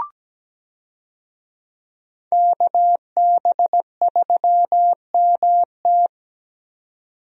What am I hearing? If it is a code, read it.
Morse code, heard as a keyed tone, about 17 wpm, 700 Hz. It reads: KB3MT
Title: USA Call Signs - 5 Characters - 17wpm